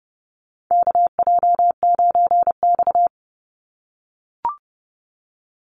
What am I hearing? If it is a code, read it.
Morse code, heard as a keyed tone, about 30 wpm, 700 Hz. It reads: KJ9X
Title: NA SOTA Chaser Callsigns 30wpm